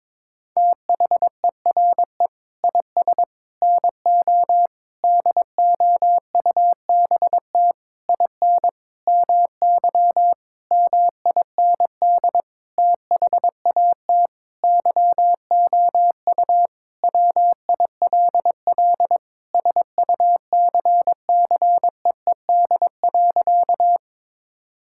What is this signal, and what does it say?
Morse code, heard as a keyed tone, about 22 wpm, 700 Hz. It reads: THERE IS NO DOUBT IN MY MIND THAT YOU WILL SUCCEED.
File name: practice / 22wpm / Sentences with 8-Word Phrases - 22wpm